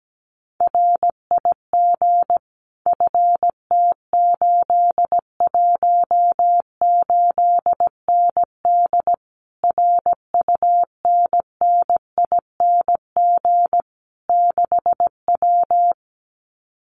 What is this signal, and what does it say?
Morse code, heard as a keyed tone, about 17 wpm, 700 Hz. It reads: RIG FT818ND RUNNING 6W
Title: Common QSO Phrases Encore 17wpm